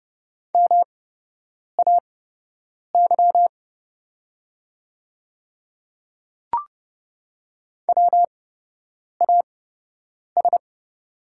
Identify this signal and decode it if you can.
Morse code, heard as a keyed tone, about 30 wpm, spaced as for 8 wpm, 700 Hz. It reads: MAY WAS
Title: 8x Character Spacing - Top 200 Words - 30wpm